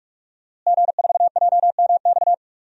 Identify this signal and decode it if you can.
Morse code, heard as a keyed tone, about 44 wpm, 700 Hz. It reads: G4JMX
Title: DXCC-100 Call Signs - 5 Characters - Repeated 3x - 45wpm